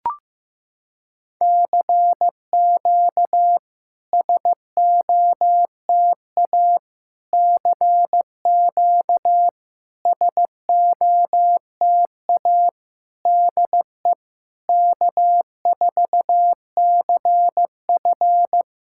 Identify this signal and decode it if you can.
Morse code, heard as a keyed tone, about 15 wpm, 700 Hz. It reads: CQ SOTA CQ SOTA DE K4CF
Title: SOTA QSO Practice 15wpm